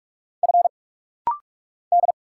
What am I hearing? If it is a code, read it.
Morse code, heard as a keyed tone, about 44 wpm, 700 Hz. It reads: F D
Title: Single Letter-Number Alt - Rapid-Fire 45wpm